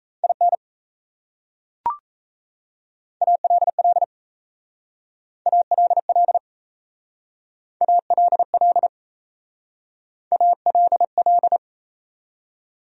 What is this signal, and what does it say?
Morse code, heard as a keyed tone, about 37 wpm, 700 Hz. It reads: IN ALL ALL ALL ALL